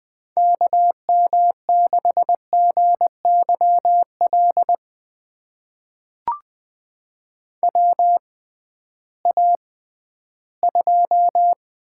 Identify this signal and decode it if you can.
Morse code, heard as a keyed tone, about 20 wpm, spaced as for 13 wpm, 700 Hz. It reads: KM6GYL WA2
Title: USA Call Signs - 6 Characters - 6x Character Spacing - 20wpm